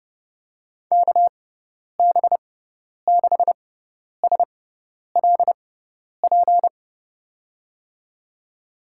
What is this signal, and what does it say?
Morse code, heard as a keyed tone, about 30 wpm, spaced as for 10 wpm, 700 Hz. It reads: KB6SLP
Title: USA Call Signs - 6x Character Spacing - 30wpm